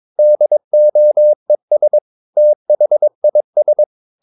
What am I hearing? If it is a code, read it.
Morse code, heard as a keyed tone, about 22 wpm, 600 Hz. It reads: DOES THIS
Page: Romanen Never af Ken Follet på engelsk.